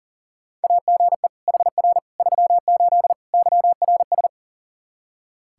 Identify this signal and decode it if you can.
Morse code, heard as a keyed tone, about 37 wpm, 700 Hz. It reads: AGE HR 38 YRS
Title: Practice QSO - Exchange 3 - More Info - 40wpm